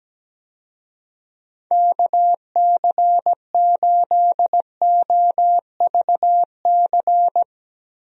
Morse code, heard as a keyed tone, about 17 wpm, 700 Hz. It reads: KC8OVC
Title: USA Call Signs - 6 Characters - 17wpm